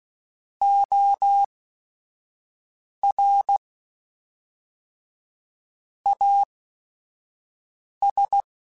Morse code, heard as a keyed tone, about 16 wpm, spaced as for 5 wpm, 800 Hz. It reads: OR AS
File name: Technical Subjects / Technical Subjects-mp3s / Reactance @5